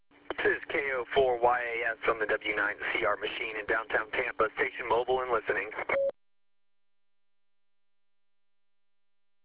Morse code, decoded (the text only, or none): none